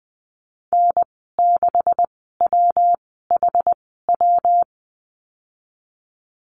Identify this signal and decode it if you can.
Morse code, heard as a keyed tone, about 20 wpm, spaced as for 15 wpm, 700 Hz. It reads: N6WHW
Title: 167 Lesson 27g - Number 7 - Call Signs 20wpm